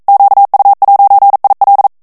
Morse code, heard as a keyed tone, about 42 wpm, 800 Hz. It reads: QU'IL
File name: CW / FICHIERS-MP3 / FRANCAIS TEXTE CLAIR 2 VITESSE 40 MOTS MINUTES